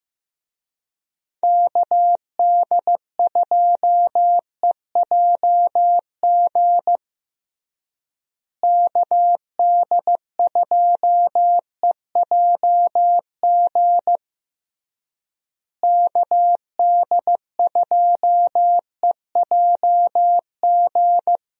Morse code, heard as a keyed tone, about 15 wpm, 700 Hz. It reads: KD2EJG KD2EJG KD2EJG
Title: USA Call Signs - 6 Characters - Repeated 3x - 15wpm